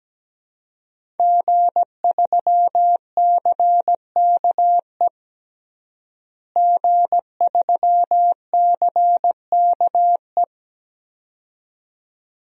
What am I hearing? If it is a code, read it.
Morse code, heard as a keyed tone, about 17 wpm, 700 Hz. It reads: G3CKE G3CKE
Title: DXCC-100 Call Signs - 5 Characters - Repeated 2x - 17wpm